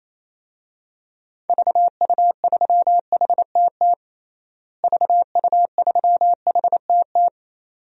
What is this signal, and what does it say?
Morse code, heard as a keyed tone, about 28 wpm, 700 Hz. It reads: VU3HTT VU3HTT
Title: DXCC-100 Call Signs - 6 Characters - Repeated 2x - 28wpm